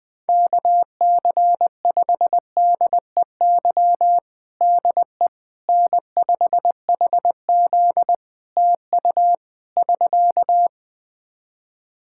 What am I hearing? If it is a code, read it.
Morse code, heard as a keyed tone, about 21 wpm, 700 Hz. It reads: KC5DEY DE N5HZ TU <SK>
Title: Practice QSO - Exchange 4 - Ending - 20wpm